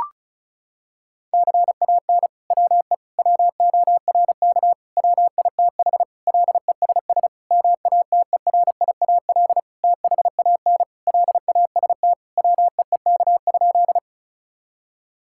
Morse code, heard as a keyed tone, about 34 wpm, 700 Hz. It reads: CAN WE WORK WITH LESS MATERIAL THAN LAST WEEK?
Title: Sentences from Top 1000 Words - Review Part 5 - 35wpm